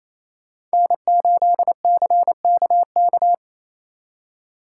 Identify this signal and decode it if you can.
Morse code, heard as a keyed tone, about 34 wpm, 700 Hz. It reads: N8CKK
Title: Speed Racing - USA Call Signs 28wpm